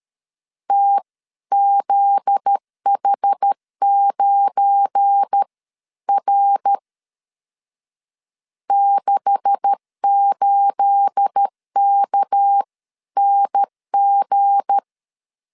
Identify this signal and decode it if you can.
Morse code, heard as a keyed tone, about 13 wpm, spaced as for 11 wpm, 800 Hz. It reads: TZH9R 68KNG
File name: CW / DET / 1 9 1500C S B 12wpm 3819